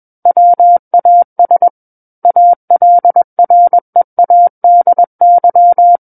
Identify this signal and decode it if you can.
Morse code, heard as a keyed tone, about 21 wpm, 700 Hz. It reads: WAS ALREADY
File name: Book One / 12-Chapter 12 - 21 WPM